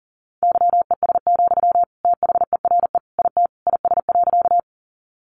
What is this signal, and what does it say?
Morse code, heard as a keyed tone, about 40 wpm, 700 Hz. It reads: YES, THERE IT IS.
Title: Sentences from Top 500 Words 40wpm